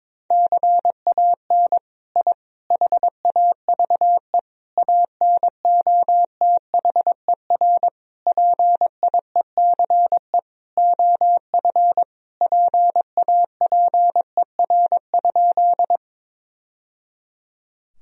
Morse code, heard as a keyed tone, about 22 wpm, 700 Hz. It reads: CAN I HAVE ANOTHER PIECE OF PAPER?
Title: Sentences from Top 1000 Words - Review Part 2 - 22wpm